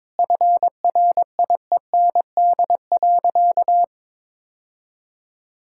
Morse code, heard as a keyed tone, about 22 wpm, 700 Hz. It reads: FRIEND.
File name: practice / 22wpm / Easy Sentences with Hints - 22wpm